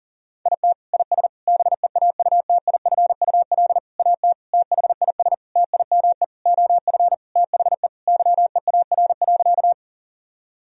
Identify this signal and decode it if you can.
Morse code, heard as a keyed tone, about 40 wpm, 700 Hz. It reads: IT IS BEAUTIFUL AT THIS TIME OF THE YEAR.
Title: Sentences from Top 1000 Words - Review Part 1 - 40wpm